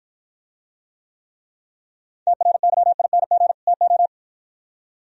Morse code, heard as a keyed tone, about 52 wpm, 700 Hz. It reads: TRYING TO